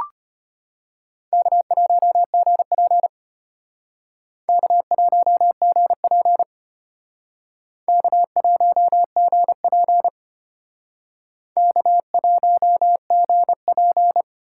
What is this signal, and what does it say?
Morse code, heard as a keyed tone, about 31 wpm, 700 Hz. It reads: K1GP K1GP K1GP K1GP